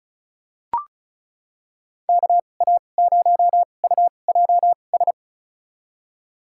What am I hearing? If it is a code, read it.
Morse code, heard as a keyed tone, about 36 wpm, spaced as for 25 wpm, 700 Hz. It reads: KA0UJS
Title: USA Call Signs - 2x Character Spacing - 35wpm